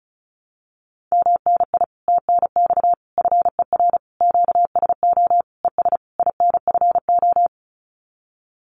Morse code, heard as a keyed tone, about 34 wpm, 700 Hz. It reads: MNI TNX FER QSO ES INFO
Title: Common QSO Phrases Encore 35wpm